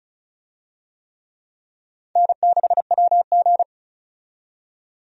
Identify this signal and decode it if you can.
Morse code, heard as a keyed tone, about 34 wpm, 700 Hz. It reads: N6WG
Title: USA Call Signs - 4 Characters - 35wpm